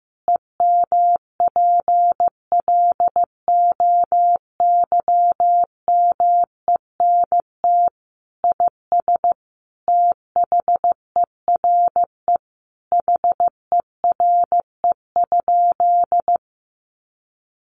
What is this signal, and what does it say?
Morse code, heard as a keyed tone, about 15 wpm, 700 Hz. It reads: EMPLOYMENT IS THERE HERE?
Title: Sentences from Top 1500 Words 15wpm